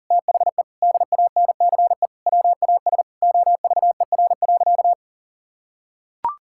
Morse code, heard as a keyed tone, about 40 wpm, 700 Hz. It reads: THE DANCE WAS OVER.
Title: Sentences from Top 1000 Words - Review Part 6 - 40wpm